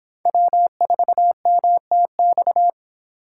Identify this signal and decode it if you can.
Morse code, heard as a keyed tone, about 26 wpm, 700 Hz. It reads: W4MTX